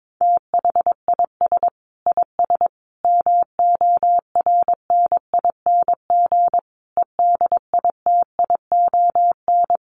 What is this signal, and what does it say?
Morse code, heard as a keyed tone, about 22 wpm, 700 Hz. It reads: THIS IS MORNING EDITION
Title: Sets of 5 Words Encore2 22wpm